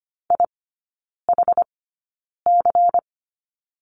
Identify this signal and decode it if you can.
Morse code, heard as a keyed tone, about 25 wpm, spaced as for 9 wpm, 700 Hz. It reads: IHC
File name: course / 25:10wpm / 093 Lesson 16e - Letter c - Three Characters 25wpm